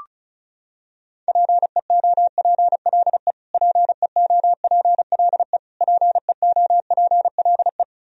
Morse code, heard as a keyed tone, about 34 wpm, 700 Hz. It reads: PEOPLE PEOPLE PEOPLE
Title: Top 100 Words Repeated 35wpm